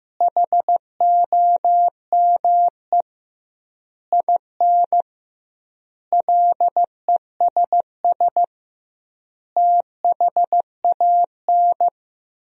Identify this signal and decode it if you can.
Morse code, heard as a keyed tone, about 15 wpm, 700 Hz. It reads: HOME IN LESS THAN
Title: 2x Word-Spacing - Sentences from Top 500 Words - 15wpm